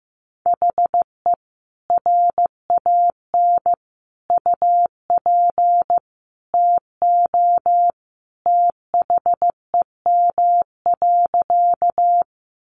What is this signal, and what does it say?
Morse code, heard as a keyed tone, about 15 wpm, 700 Hz. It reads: HE RAN UP TO THEM.
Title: Sentences from Top 500 Words 15wpm